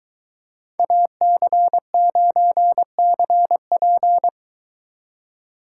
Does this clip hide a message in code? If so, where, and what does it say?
Yes: Morse code from 0.8 to 4.3 s, heard as a keyed tone, about 23 wpm, 700 Hz. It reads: AC9CP